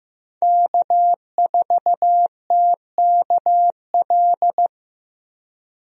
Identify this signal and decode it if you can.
Morse code, heard as a keyed tone, about 15 wpm, 700 Hz. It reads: K4TKL
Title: USA Call Signs - 5 Characters - Repeated 2x - 15wpm